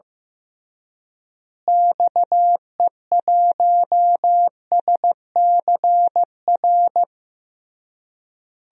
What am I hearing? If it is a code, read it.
Morse code, heard as a keyed tone, about 15 wpm, 700 Hz. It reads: XE1SCR
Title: DXCC-100 Call Signs - 6 Characters - Repeated 3x - 15wpm